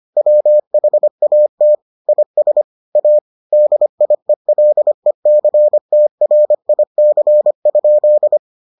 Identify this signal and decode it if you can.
Morse code, heard as a keyed tone, about 25 wpm, 600 Hz. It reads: WHAT IS A DIELECTRIC?